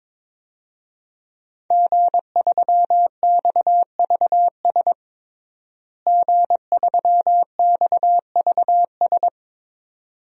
Morse code, heard as a keyed tone, about 22 wpm, 700 Hz. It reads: G3XVS G3XVS
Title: DXCC-100 Call Signs - 5 Characters - Repeated 2x - 22wpm